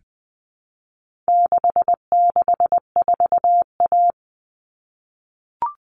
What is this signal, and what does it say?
Morse code, heard as a keyed tone, about 20 wpm, 700 Hz. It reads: 664A
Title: All Cut Numbers - 4-Digit 20wpm